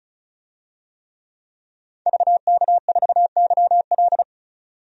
Morse code, heard as a keyed tone, about 36 wpm, 700 Hz. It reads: VK4YL
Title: DXCC-100 Call Signs - 5 Characters - 35wpm